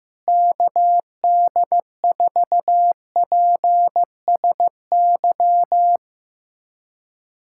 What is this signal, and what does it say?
Morse code, heard as a keyed tone, about 15 wpm, 700 Hz. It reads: KD4PSY
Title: USA Callsign DE Callsign 15wpm